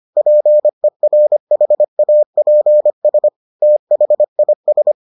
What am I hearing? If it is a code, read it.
Morse code, heard as a keyed tone, about 25 wpm, 600 Hz. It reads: PERHAPS THIS